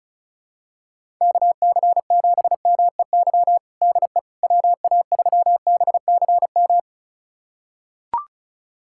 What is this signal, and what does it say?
Morse code, heard as a keyed tone, about 36 wpm, 700 Hz. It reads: KC7MEY DE WA3BCM
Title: USA Callsign DE Callsign 35wpm